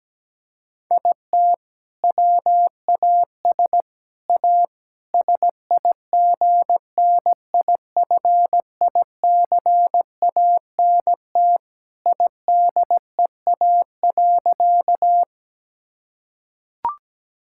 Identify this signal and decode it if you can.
Morse code, heard as a keyed tone, about 17 wpm, 700 Hz. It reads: IT WAS A SIGNIFICANT IDEA.
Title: Sentences from Top 1000 Words - Review Part 6 - 17wpm